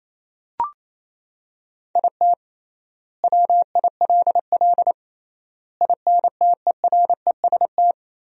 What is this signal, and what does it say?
Morse code, heard as a keyed tone, about 28 wpm, 700 Hz. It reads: IT WILL INTEREST
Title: 3x Word-Spacing - Sentences from Top 400 Words - 28wpm